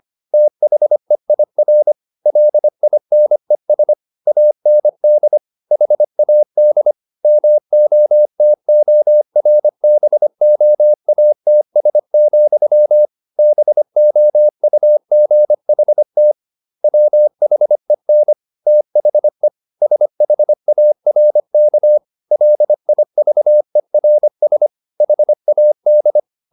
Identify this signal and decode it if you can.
Morse code, heard as a keyed tone, about 25 wpm, 600 Hz. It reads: THEIR LINES AND HAD MOTORBOATS, BOUGHT WHEN THE SHARK LIVERS HAD
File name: the old man and the sea part 1 25wpm